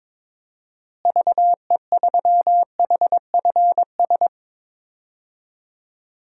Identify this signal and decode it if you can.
Morse code, heard as a keyed tone, about 22 wpm, 700 Hz. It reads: VE3HFS
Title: DXCC-100 Call Signs - 6 Characters - Repeated 3x - 22wpm